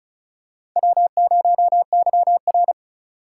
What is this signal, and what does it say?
Morse code, heard as a keyed tone, about 36 wpm, 700 Hz. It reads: W0YR
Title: K1USN SST QSO Practice - 35wpm